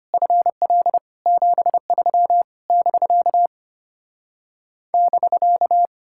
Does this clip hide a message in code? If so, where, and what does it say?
Yes: Morse code from 0.1 to 5.9 s, heard as a keyed tone, about 33 wpm, 700 Hz. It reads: FL 73 <BK> <BK>